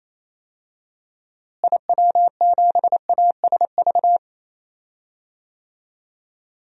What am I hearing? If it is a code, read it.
Morse code, heard as a keyed tone, about 28 wpm, 700 Hz. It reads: IW7ASV
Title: DXCC-100 Call Signs - 6 Characters - Repeated 2x - 28wpm